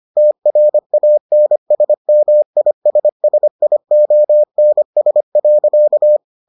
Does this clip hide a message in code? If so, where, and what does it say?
Yes: Morse code from 0.2 to 6.2 s, heard as a keyed tone, about 25 wpm, 600 Hz. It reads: TRANSMISSIONS.